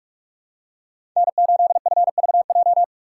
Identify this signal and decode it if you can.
Morse code, heard as a keyed tone, about 44 wpm, 700 Hz. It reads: N8FVJ